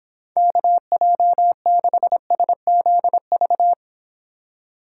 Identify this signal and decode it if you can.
Morse code, heard as a keyed tone, about 26 wpm, 700 Hz. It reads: KJ6SZV